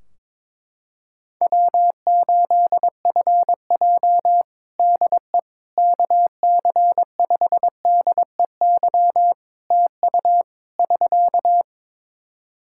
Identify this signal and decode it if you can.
Morse code, heard as a keyed tone, about 22 wpm, 700 Hz. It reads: W8FJ DE KC5DEY TU <SK>
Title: Practice QSO - Exchange 4 - Ending - 22wpm